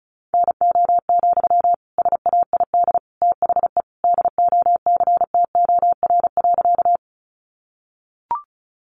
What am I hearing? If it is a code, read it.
Morse code, heard as a keyed tone, about 34 wpm, 700 Hz. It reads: NO, SAID THE DOCTOR.
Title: Sentences from Top 1500 Words 35wpm